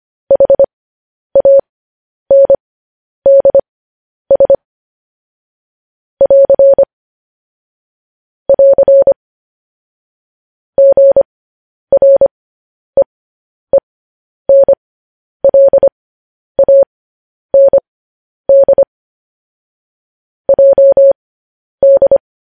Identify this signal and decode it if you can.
Morse code, heard as a keyed tone, about 25 wpm, spaced as for 10 wpm, 550 Hz. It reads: HANDS <AR> <AR> GREENLAND JD
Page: news-headlines-in-morse-code-at-10-wpm